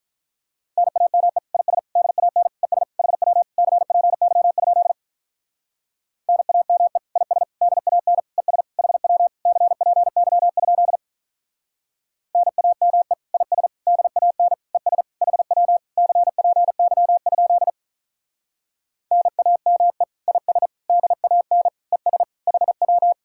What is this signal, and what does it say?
Morse code, heard as a keyed tone, about 44 wpm, 700 Hz. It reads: NAME IS DAN ES HW CPY? NAME IS DAN ES HW CPY? NAME IS DAN ES HW CPY? NAME IS DAN ES HW